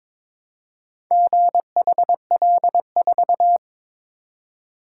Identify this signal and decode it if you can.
Morse code, heard as a keyed tone, about 23 wpm, 700 Hz. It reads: GHL4
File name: practice / 22wpm / Four Letter-Number Combinations - 22wpm